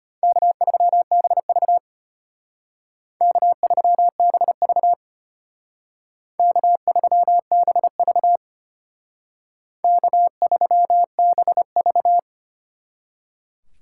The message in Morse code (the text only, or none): K3BV K3BV K3BV K3BV